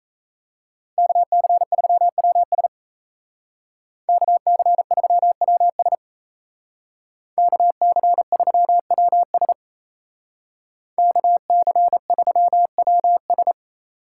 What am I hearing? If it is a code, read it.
Morse code, heard as a keyed tone, about 32 wpm, 700 Hz. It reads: KC3WS KC3WS KC3WS KC3WS
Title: Speed Racing - USA 2x2 Call Signs 28wpm